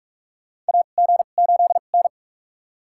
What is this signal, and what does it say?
Morse code, heard as a keyed tone, about 42 wpm, spaced as for 31 wpm, 700 Hz. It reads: AG8N
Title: USA Call Signs - 2x Character Spacing - 45wpm